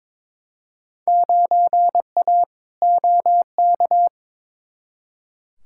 Morse code, heard as a keyed tone, about 22 wpm, 700 Hz. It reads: 9A OK
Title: Field Day Exchanges Encore - 22wpm